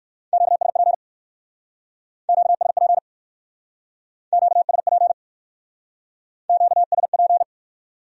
Morse code, heard as a keyed tone, about 57 wpm, 700 Hz. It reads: QSP QSP QSP QSP